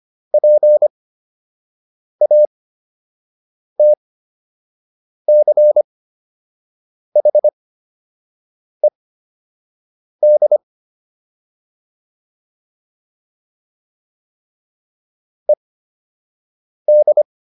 Morse code, heard as a keyed tone, about 25 wpm, spaced as for 6 wpm, 600 Hz. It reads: PATCHED ED